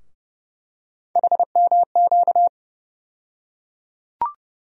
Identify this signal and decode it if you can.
Morse code, heard as a keyed tone, about 30 wpm, 700 Hz. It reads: HMQ
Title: USA Call Sign Postfixes - 3 Letters 30wpm